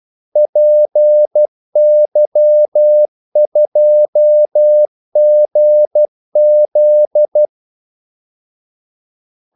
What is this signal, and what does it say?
Morse code, heard as a keyed tone, about 12 wpm, 600 Hz. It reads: PY2GZ
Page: 2 ekstra ordmellemrum mellem kaldesignalerne